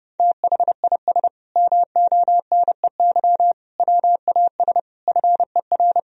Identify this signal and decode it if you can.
Morse code, heard as a keyed tone, about 32 wpm, 700 Hz. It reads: THIS MONEY WAS FER
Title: Sentences from Top 400 Words and Abbreviations 30wpm